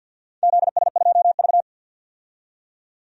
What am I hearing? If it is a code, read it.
Morse code, heard as a keyed tone, about 49 wpm, 700 Hz. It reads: ZS2V
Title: DXCC-100 Call Signs - 4 Characters - 50wpm